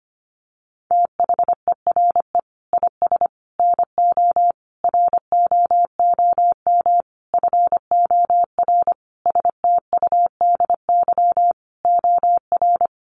Morse code, heard as a keyed tone, about 25 wpm, 700 Hz. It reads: THERE IS NO ROOM FOR STUDY OR
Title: Sentences from Top 1200 Words 25wpm